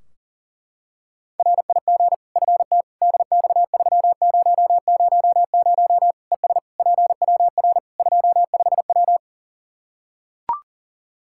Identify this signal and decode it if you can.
Morse code, heard as a keyed tone, about 37 wpm, 700 Hz. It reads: RIG FT DX3000 ES PWR 25W
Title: K1USN SST QSO Practice 40wpm